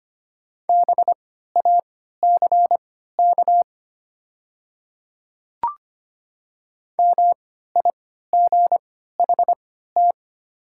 Morse code, heard as a keyed tone, about 25 wpm, spaced as for 14 wpm, 690 Hz. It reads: BACK MIGHT
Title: 3x Character Spacing - Top 500 Words - 25wpm